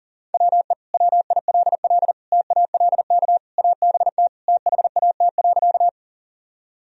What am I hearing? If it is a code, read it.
Morse code, heard as a keyed tone, about 40 wpm, 700 Hz. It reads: WE WILL TALK ABT THAT.
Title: Sentences from Top 400 Words and Abbreviations 40wpm